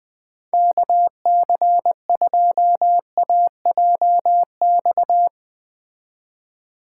Morse code, heard as a keyed tone, about 20 wpm, 700 Hz. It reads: KC2AJX
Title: Speed Racing - USA 2x3 Call Signs 17wpm